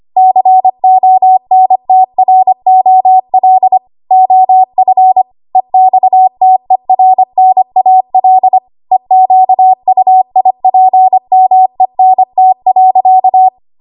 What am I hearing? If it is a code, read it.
Morse code, heard as a keyed tone, about 25 wpm, 750 Hz. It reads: CONTROL OF EXTERNAL EQUIPMENT.
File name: Archive / 25 WPM / 25WPM